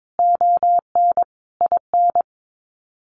Morse code, heard as a keyed tone, about 22 wpm, 700 Hz. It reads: ON IN
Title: Sets of 5 Words Encore2 22wpm